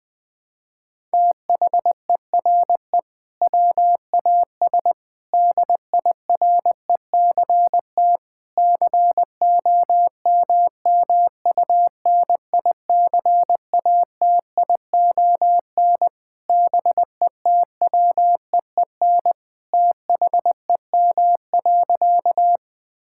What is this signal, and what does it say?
Morse code, heard as a keyed tone, about 19 wpm, 700 Hz. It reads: THERE WAS DIRECT COMMUNICATION BETWEEN THEM.
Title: Sentences from Top 1000 Words - Review Part 6 - 20wpm